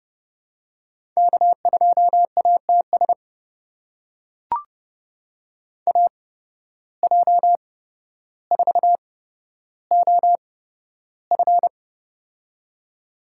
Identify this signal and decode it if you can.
Morse code, heard as a keyed tone, about 29 wpm, spaced as for 13 wpm, 700 Hz. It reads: K2ATS AJ4OF
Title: USA Call Signs - 5 Characters - 8x Character Spacing - 30wpm